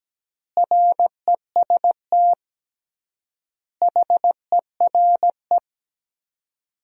Morse code, heard as a keyed tone, about 17 wpm, 700 Hz. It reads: REST HERE
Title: 3x Word-Spacing - Sentences from Top 500 Words - 17wpm